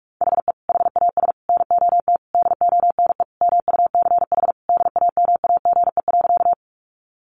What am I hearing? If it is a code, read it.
Morse code, heard as a keyed tone, about 46 wpm, 700 Hz. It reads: HE HAS NOT DONE MUCH DAMAGE.
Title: Sentences from Top 1100 Words 45wpm